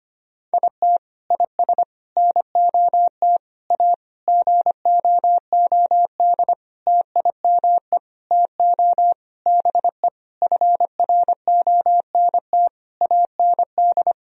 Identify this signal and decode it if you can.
Morse code, heard as a keyed tone, about 25 wpm, 700 Hz. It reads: IT IS NOT A GOOD TIME TO BE FRONT AND
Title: Sentences from Top 1000 Words - Review Part 2 - 25wpm